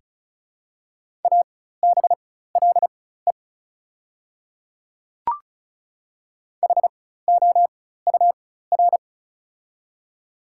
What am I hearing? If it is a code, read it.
Morse code, heard as a keyed tone, about 36 wpm, spaced as for 16 wpm, 690 Hz. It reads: ABLE HOUR
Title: 4x Character Spacing - Top 500 Words - 35wpm